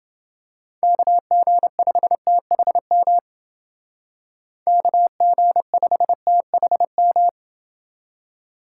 Morse code, heard as a keyed tone, about 28 wpm, 700 Hz. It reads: KG5THM KG5THM